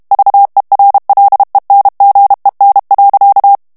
Morse code, heard as a keyed tone, about 32 wpm, 800 Hz. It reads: VERLENGEN.
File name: HOLLANDAIS texte clair 2 vitesse 30 mots minute